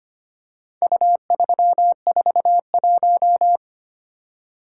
Morse code, heard as a keyed tone, about 25 wpm, 700 Hz. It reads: U341
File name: practice / 25wpm / Four Letter-Number Combinations - 25wpm